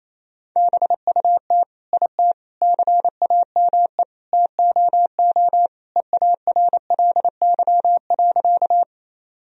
Morse code, heard as a keyed tone, about 28 wpm, 700 Hz. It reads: BUT IT CAME TOO EARLY.
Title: Sentences from Top 1000 Words - Review Part 2 - 28wpm